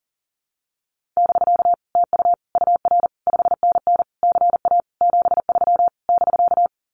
Morse code, heard as a keyed tone, about 40 wpm, 700 Hz. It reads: <BK> TU UR 5NN CA 73 <BK>
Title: Common QSO Phrases Encore 40wpm